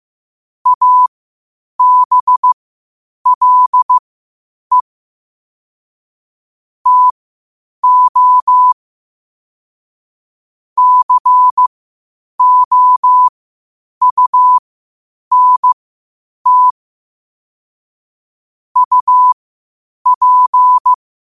Morse code, heard as a keyed tone, about 15 wpm, spaced as for 8 wpm, 1000 Hz. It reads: ABLE TO COUNT UP
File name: qotdfiles / 8wpm